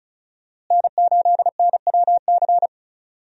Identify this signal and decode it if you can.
Morse code, heard as a keyed tone, about 36 wpm, 700 Hz. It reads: N8NWC
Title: USA Call Signs - 5 Characters - Repeated 3x - 35wpm